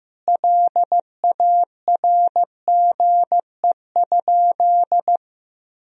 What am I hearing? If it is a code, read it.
Morse code, heard as a keyed tone, about 15 wpm, 700 Hz. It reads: LARGE?
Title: Sentences from Top 1000 Words - Review Part 1 - 15wpm